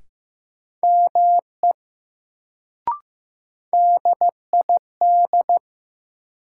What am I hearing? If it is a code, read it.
Morse code, heard as a keyed tone, about 15 wpm, 700 Hz. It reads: ME DID
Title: Sets of 2 Words - New Word Drill - 15wpm